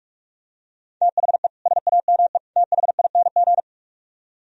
Morse code, heard as a keyed tone, about 49 wpm, 700 Hz. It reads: THE SAME THING